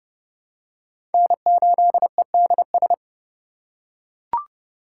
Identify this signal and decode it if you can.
Morse code, heard as a keyed tone, about 29 wpm, 700 Hz. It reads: N8EDS
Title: USA Call Signs - 4x Character Spacing - 30wpm